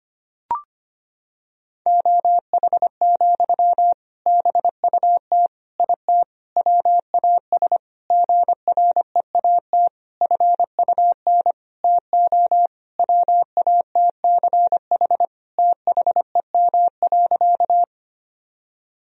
Morse code, heard as a keyed tone, about 25 wpm, 700 Hz. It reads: OH, BUT IT WAS GREAT FUN TO WATCH THEM.
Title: Sentences from Top 1000 Words - Review Part 4 - 25wpm